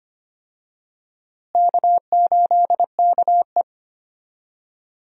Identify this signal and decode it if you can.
Morse code, heard as a keyed tone, about 25 wpm, 700 Hz. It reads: K8KE
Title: USA Call Signs - 4 Characters - Repeated 3x - 25wpm